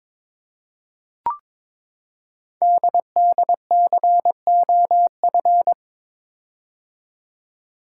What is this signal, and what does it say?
Morse code, heard as a keyed tone, about 22 wpm, 700 Hz. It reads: DDCOF
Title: Code Groups - 5 Letters - 22wpm